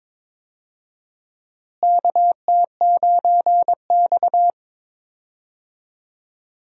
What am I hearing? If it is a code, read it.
Morse code, heard as a keyed tone, about 22 wpm, 700 Hz. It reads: KT9X
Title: USA Call Signs - 4 Characters - Repeated 2x - 22wpm